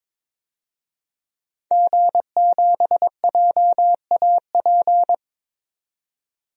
Morse code, heard as a keyed tone, about 22 wpm, 700 Hz. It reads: G7JAP
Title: DXCC-100 Call Signs - 5 Characters - 22wpm